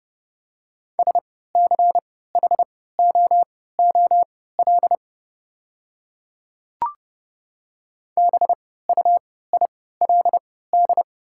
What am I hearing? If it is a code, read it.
Morse code, heard as a keyed tone, about 30 wpm, spaced as for 17 wpm, 690 Hz. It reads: SCHOOL BUILD
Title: 3x Character Spacing - Top 300 Words - 30wpm